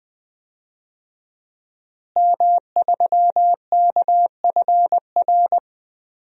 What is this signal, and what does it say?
Morse code, heard as a keyed tone, about 20 wpm, 700 Hz. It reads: M3KFR
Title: DXCC-100 Call Signs - 5 Characters - Repeated 2x - 20wpm